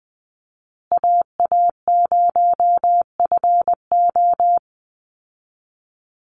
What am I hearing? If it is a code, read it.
Morse code, heard as a keyed tone, about 20 wpm, 700 Hz. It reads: AA0FO
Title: Call Signs Repeated 3x - 20wpm